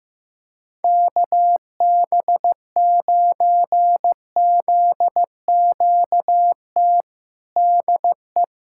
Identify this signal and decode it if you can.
Morse code, heard as a keyed tone, about 15 wpm, 700 Hz. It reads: KB9ZQT DE
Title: USA Callsign DE Callsign 15wpm